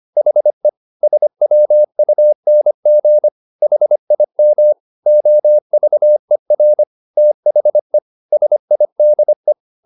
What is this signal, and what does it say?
Morse code, heard as a keyed tone, about 25 wpm, 600 Hz. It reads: HE SWUNG HIM OVER THE SIDE
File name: the old man and the sea part 1 25wpm